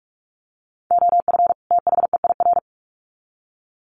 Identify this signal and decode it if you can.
Morse code, heard as a keyed tone, about 44 wpm, 700 Hz. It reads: OF THEIR
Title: Sets of 2 Words 45wpm